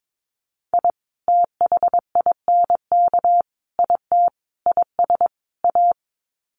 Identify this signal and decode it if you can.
Morse code, heard as a keyed tone, about 22 wpm, 700 Hz. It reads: I THINK IT IS A
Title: Sets of 5 Words Encore 22wpm